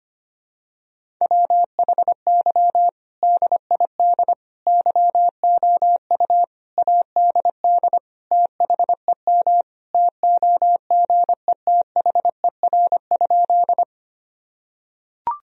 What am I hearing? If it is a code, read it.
Morse code, heard as a keyed tone, about 25 wpm, 700 Hz. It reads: WHY DID YOU ADD THEM TOGETHER?
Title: Sentences from Top 1000 Words - Review Part 1 - 25wpm